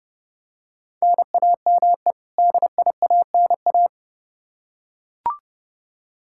Morse code, heard as a keyed tone, about 30 wpm, 700 Hz. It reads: NAME DIANA
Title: Practice QSO - Exchange 1 - Beginning - 30wpm